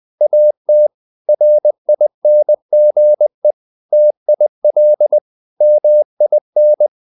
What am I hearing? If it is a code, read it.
Morse code, heard as a keyed tone, about 20 wpm, 600 Hz. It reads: AT RINGE TIL MIN
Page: Ferie 20 wpm